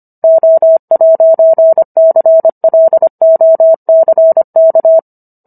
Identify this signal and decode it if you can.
Morse code, heard as a keyed tone, about 25 wpm, 650 Hz. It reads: O'CLOCK